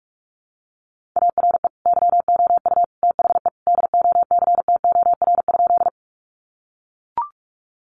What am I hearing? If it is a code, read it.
Morse code, heard as a keyed tone, about 44 wpm, 700 Hz. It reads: ARE YOU THE DOCTOR?
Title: Sentences from Top 1500 Words 45wpm